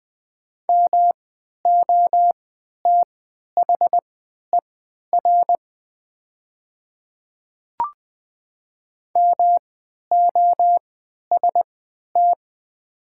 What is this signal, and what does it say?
Morse code, heard as a keyed tone, about 20 wpm, spaced as for 11 wpm, 700 Hz. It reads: MOTHER MOST
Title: 3x Character Spacing - Top 300 Words - 20wpm